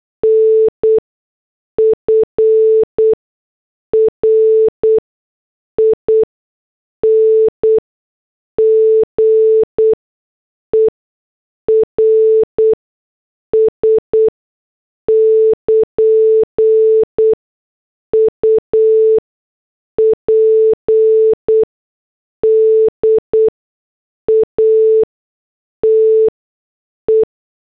Morse code, heard as a keyed tone, about 8 wpm, 440 Hz. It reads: N F R I N G E R S <KN> U P D A T E
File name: morse / headlines